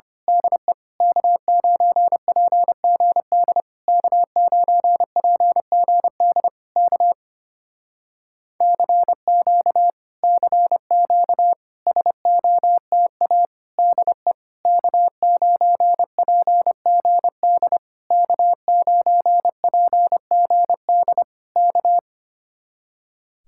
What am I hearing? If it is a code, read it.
Morse code, heard as a keyed tone, about 25 wpm, 700 Hz. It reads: DE K9PGD K9PGD K CQ CQ SOTA DE K9PGD K9PGD K
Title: Speed-Racing Common QSO Phrases - 25wpm